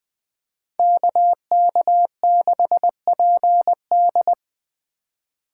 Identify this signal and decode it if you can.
Morse code, heard as a keyed tone, about 20 wpm, 700 Hz. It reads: KK6PD
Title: USA Call Signs - 5 Characters - 4x Character Spacing - 20wpm